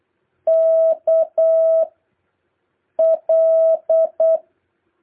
Morse code, heard as a keyed tone, about 8 wpm, 650 Hz. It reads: K L